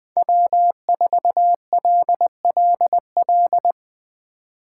Morse code, heard as a keyed tone, about 21 wpm, 700 Hz. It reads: W4LLL
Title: USA Call Signs - 5 Characters - 4x Character Spacing - 20wpm